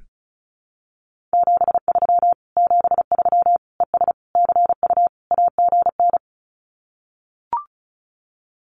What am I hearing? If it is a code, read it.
Morse code, heard as a keyed tone, about 34 wpm, 700 Hz. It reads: 73 73 ES CU AGN
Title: Common QSO Phrases Encore 35wpm